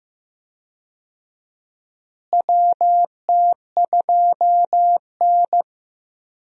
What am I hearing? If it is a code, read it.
Morse code, heard as a keyed tone, about 15 wpm, 700 Hz. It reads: WT2N
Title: USA Call Signs - 4 Characters - Repeated 2x - 15wpm